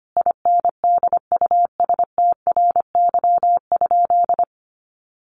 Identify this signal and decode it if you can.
Morse code, heard as a keyed tone, about 25 wpm, 700 Hz. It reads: INDUSTRY?
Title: Sentences from Top 1300 Words 25wpm